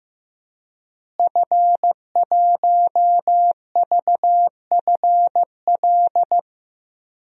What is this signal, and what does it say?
Morse code, heard as a keyed tone, about 15 wpm, 700 Hz. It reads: F1VFL
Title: DXCC-100 Call Signs - 5 Characters - Repeated 3x - 15wpm